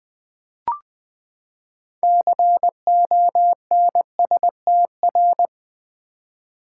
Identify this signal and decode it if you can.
Morse code, heard as a keyed tone, about 20 wpm, 700 Hz. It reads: CONSTR
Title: Build A Word - 6 Letters At A Time - 12-Letter Words - 20wpm